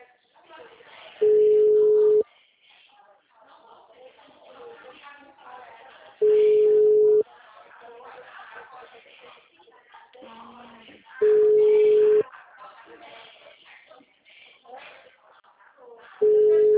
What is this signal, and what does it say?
Ringback tone (Europe, 425 Hz), with speech.